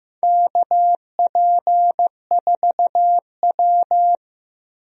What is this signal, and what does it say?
Morse code, heard as a keyed tone, about 15 wpm, 700 Hz. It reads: KP4W
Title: DXCC-100 Call Signs - 4 Characters - Repeated 3x - 15wpm